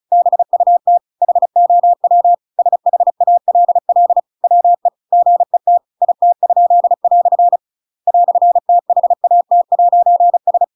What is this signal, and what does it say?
Morse code, heard as a keyed tone, about 34 wpm, 700 Hz. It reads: BUT HOW SHALL WE GET IT?" "THAT'S
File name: Pinocchio / Pinocchio ch8 35 35 ik1hge